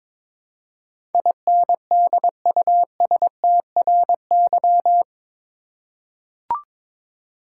Morse code, heard as a keyed tone, about 22 wpm, 700 Hz. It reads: INDUSTRY
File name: practice / 22wpm / Eight Letter Words - 22wpm